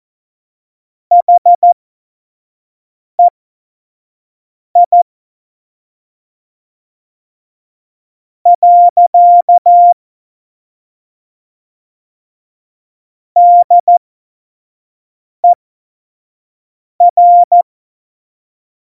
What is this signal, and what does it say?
Morse code, heard as a keyed tone, about 14 wpm, spaced as for 5 wpm, 700 Hz. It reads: HEI . DER
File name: Morsekurs / Witze / Witz 26. Juni 2018, 25 F70